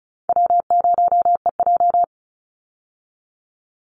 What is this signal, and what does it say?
Morse code, heard as a keyed tone, about 34 wpm, 700 Hz. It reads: W0EJ
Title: Easy Call Signs Repeated 3x - 35wpm